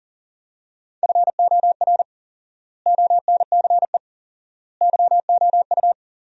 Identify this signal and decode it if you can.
Morse code, heard as a keyed tone, about 40 wpm, 700 Hz. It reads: FOR ONCE YOU
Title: 4x Word-Spacing - Sentences from Top 300 Words - 40wpm